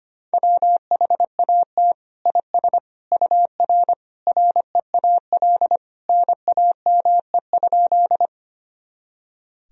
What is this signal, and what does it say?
Morse code, heard as a keyed tone, about 25 wpm, 700 Hz. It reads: WHAT IS UR REAL NAME?
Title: Sentences from Top 300 Words and Abbreviations 25wpm